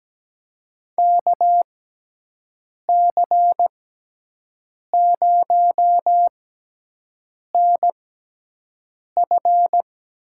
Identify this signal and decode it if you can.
Morse code, heard as a keyed tone, about 17 wpm, spaced as for 6 wpm, 700 Hz. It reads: KC0NF
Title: USA Call Signs - 6 Characters - 6x Character Spacing - 17wpm